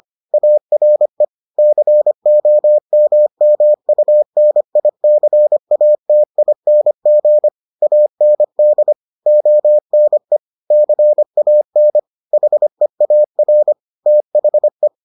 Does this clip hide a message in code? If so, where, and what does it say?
Yes: Morse code from 0.3 to 14.9 s, heard as a keyed tone, about 25 wpm, 600 Hz. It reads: ARE COMMUNICATING AND ONE CAN HEAR THE